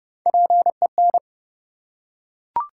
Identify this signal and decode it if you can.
Morse code, heard as a keyed tone, about 30 wpm, 700 Hz. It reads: PEN
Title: Three Letter Words - 30wpm